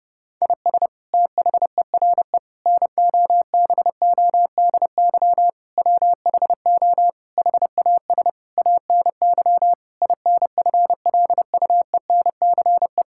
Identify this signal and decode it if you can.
Morse code, heard as a keyed tone, about 30 wpm, 700 Hz. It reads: IS THERE NOBODY WHO HAS ANY INFLUENCE
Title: Sentences from Top 1600 Words 30wpm